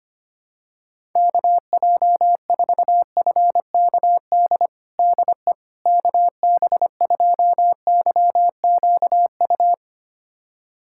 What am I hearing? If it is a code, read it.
Morse code, heard as a keyed tone, about 25 wpm, 700 Hz. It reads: KJ4FKD DE KB2YQU
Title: USA Callsign DE Callsign 25wpm